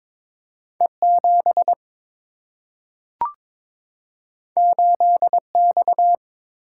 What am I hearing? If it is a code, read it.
Morse code, heard as a keyed tone, about 22 wpm, 700 Hz. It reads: E 7 8 X
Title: Single-Letter Single-Number 22wpm